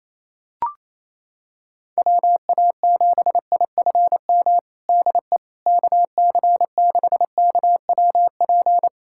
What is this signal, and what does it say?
Morse code, heard as a keyed tone, about 28 wpm, 700 Hz. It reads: WA7IFM DE KC6KWP
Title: USA Callsign DE Callsign - 6 Characters - 28wpm